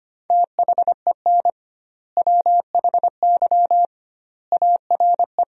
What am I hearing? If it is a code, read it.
Morse code, heard as a keyed tone, about 25 wpm, 700 Hz. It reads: THEN WHY ARE
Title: 2x Word-Spacing - Sentences from Top 200 Words - 25wpm